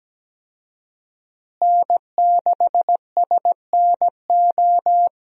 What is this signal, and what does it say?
Morse code, heard as a keyed tone, about 17 wpm, 700 Hz. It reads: N6SNO
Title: USA Call Signs - 5 Characters - Repeated 2x - 17wpm